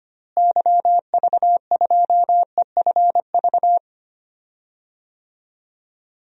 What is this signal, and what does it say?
Morse code, heard as a keyed tone, about 25 wpm, 700 Hz. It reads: YV2EFV
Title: DXCC-100 Call Signs - 6 Characters - Repeated 2x - 25wpm